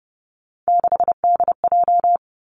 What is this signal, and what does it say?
Morse code, heard as a keyed tone, about 30 wpm, 700 Hz. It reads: 6DJ
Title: Three Challenging Letter-Number Combinations 30wpm